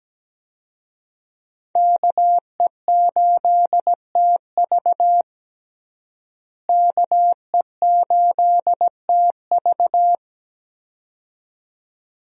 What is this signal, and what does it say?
Morse code, heard as a keyed tone, about 17 wpm, 700 Hz. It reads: KE8TV KE8TV
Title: USA Call Signs - 5 Characters - Repeated 2x - 17wpm